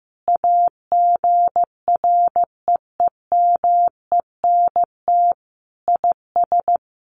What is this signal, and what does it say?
Morse code, heard as a keyed tone, about 15 wpm, 700 Hz. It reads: AGREEMENT IS
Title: Sentences from Top 1500 Words 15wpm